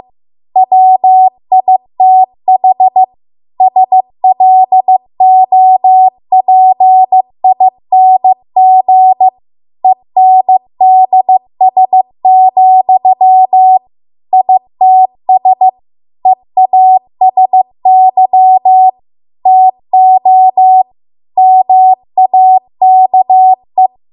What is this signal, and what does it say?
Morse code, heard as a keyed tone, about 15 wpm, 750 Hz. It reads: WITH SLOPING ENDS, ITS EASY TO MAKE